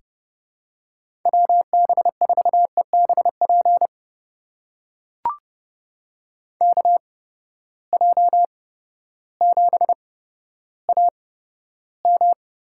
Morse code, heard as a keyed tone, about 30 wpm, spaced as for 14 wpm, 700 Hz. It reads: WB4EBP KJ7AM
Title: USA Call Signs - 6 Characters - 8x Character Spacing - 30wpm